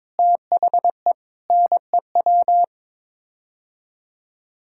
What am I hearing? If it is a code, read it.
Morse code, heard as a keyed tone, about 22 wpm, 700 Hz. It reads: THE NEW
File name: practice / 22wpm / Sets of 8 Words - 22wpm